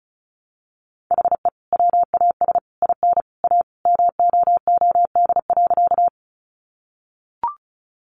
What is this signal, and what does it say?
Morse code, heard as a keyed tone, about 34 wpm, 700 Hz. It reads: HE WAS IN A MOOD.
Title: Sentences from Top 1100 Words 35wpm